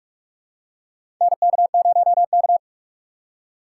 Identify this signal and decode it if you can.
Morse code, heard as a keyed tone, about 46 wpm, 700 Hz. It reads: NK0K
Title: USA Call Signs - 4 Characters - 3x Character Spacing - 45wpm